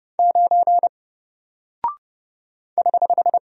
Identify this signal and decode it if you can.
Morse code, heard as a keyed tone, about 30 wpm, 700 Hz. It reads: 9 <HH>
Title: Single Character (Letter:Number:Punctuation:Prosign) Alt 30wpm